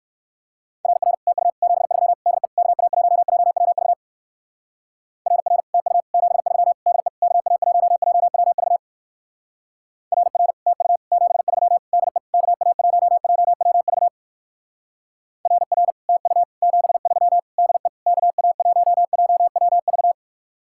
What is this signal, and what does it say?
Morse code, heard as a keyed tone, about 58 wpm, 700 Hz. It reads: RR TU 73 DE KA1JWV RR TU 73 DE KA1JWV RR TU 73 DE KA1JWV RR TU 73 DE KA1JWV